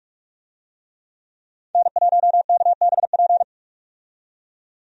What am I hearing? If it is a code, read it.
Morse code, heard as a keyed tone, about 43 wpm, 700 Hz. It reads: N1KBP
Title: USA Call Signs - 5 Characters - 45wpm